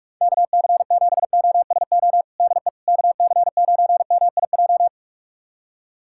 Morse code, heard as a keyed tone, about 42 wpm, 700 Hz. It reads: KC7OSO DE KC9MIJ
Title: USA Callsign DE Callsign - 6 Characters - 45wpm